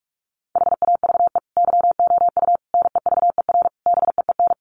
Morse code, heard as a keyed tone, about 44 wpm, 700 Hz. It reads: HAVE YOU NEVER BEEN
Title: Sentences from Top 1400 Words 45wpm